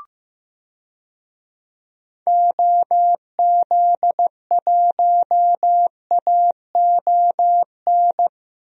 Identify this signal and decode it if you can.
Morse code, heard as a keyed tone, about 15 wpm, 700 Hz. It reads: OZ1AON
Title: DXCC-100 Call Signs - 6 Characters - Repeated 3x - 15wpm